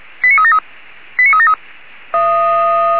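Five-tone selective call (ZVEI), heard as quick runs of five tones, tones of about 80 ms. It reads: 89373 89373
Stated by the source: Five-tone ZVEI selective call received by SDR, from the audio server of Feuerwehr Orlamünde.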